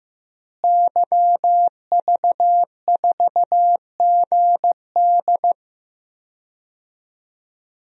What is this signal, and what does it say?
Morse code, heard as a keyed tone, about 15 wpm, 700 Hz. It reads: YV4GD
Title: DXCC-100 Call Signs - 5 Characters - Repeated 2x - 15wpm